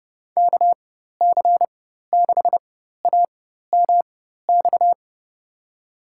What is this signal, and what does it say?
Morse code, heard as a keyed tone, about 30 wpm, spaced as for 14 wpm, 700 Hz. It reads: KC6AMX
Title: USA Call Signs - 4x Character Spacing - 30wpm